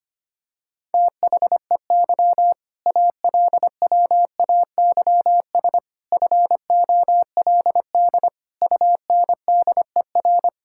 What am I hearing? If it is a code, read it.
Morse code, heard as a keyed tone, about 25 wpm, 700 Hz. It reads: THEY ALWAYS FOLD UNDER
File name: practice / 25wpm / All-Age Appropriate Jokes - 25wpm